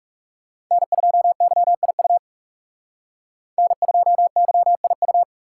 Morse code, heard as a keyed tone, about 38 wpm, 700 Hz. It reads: N2YIU N2YIU